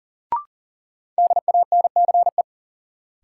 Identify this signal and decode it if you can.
Morse code, heard as a keyed tone, about 40 wpm, 700 Hz. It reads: DANCE
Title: Sentences from Top 700 Words - New Word Drill - 40wpm